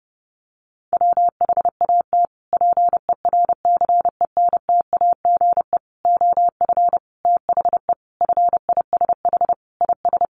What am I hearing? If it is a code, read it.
Morse code, heard as a keyed tone, about 28 wpm, 700 Hz. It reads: WHAT PERCENTAGE OF THE FISH IS
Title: Sentences from Top 1300 Words 30wpm